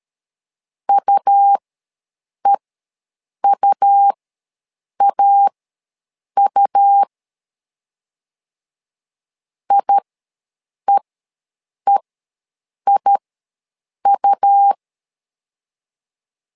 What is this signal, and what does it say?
Morse code, heard as a keyed tone, about 13 wpm, spaced as for 7 wpm, 800 Hz. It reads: UEUAU IEEIU
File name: LEZIONI / 2 UHDN / 2 U ws 12wpm 1655